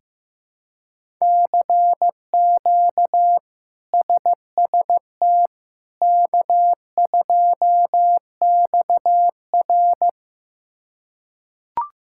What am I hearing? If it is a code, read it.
Morse code, heard as a keyed tone, about 15 wpm, 700 Hz. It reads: CQ SST K2XR
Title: K1USN SST QSO Practice - 15wpm